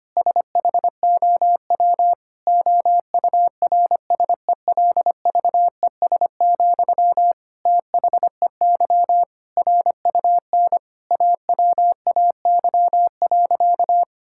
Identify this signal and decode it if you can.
Morse code, heard as a keyed tone, about 25 wpm, 700 Hz. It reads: SHOW OURSELVES, THEY RUN AWAY.
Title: Sentences from Top 1600 Words 25wpm